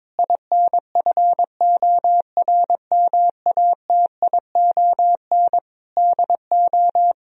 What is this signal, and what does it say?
Morse code, heard as a keyed tone, about 22 wpm, 700 Hz. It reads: INFORMATION DO
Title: Sentences from Top 1000 Words - Review Part 3 - 22wpm